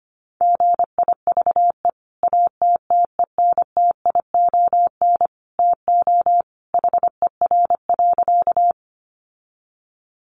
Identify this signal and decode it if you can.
Morse code, heard as a keyed tone, about 25 wpm, 700 Hz. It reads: GIVE ATTENTION TO HER.
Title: Sentences from Top 1000 Words 25wpm